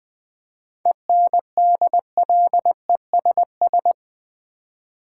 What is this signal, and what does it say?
Morse code, heard as a keyed tone, about 20 wpm, 700 Hz. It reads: ENDLESS
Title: 7 Letter Words - First 3 Letters Given - 20wpm